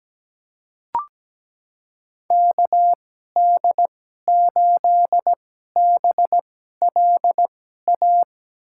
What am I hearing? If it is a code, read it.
Morse code, heard as a keyed tone, about 17 wpm, spaced as for 12 wpm, 700 Hz. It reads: KD8BLA
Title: USA Call Signs - 6 Characters - 2x Character Spacing - 17wpm